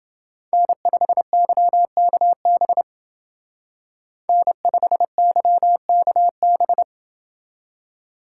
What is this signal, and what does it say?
Morse code, heard as a keyed tone, about 30 wpm, 700 Hz. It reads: N5YKB N5YKB